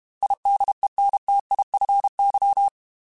Morse code, heard as a keyed tone, about 32 wpm, 800 Hz. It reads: IDENTIFY
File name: Technical Subjects / Technical Subjects-mp3s / Sw Pwr Sup @30